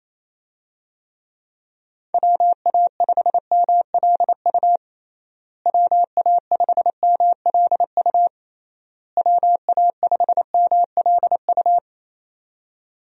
Morse code, heard as a keyed tone, about 28 wpm, 700 Hz. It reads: WA5MLU WA5MLU WA5MLU
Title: USA Call Signs - 6 Characters - Repeated 3x - 28wpm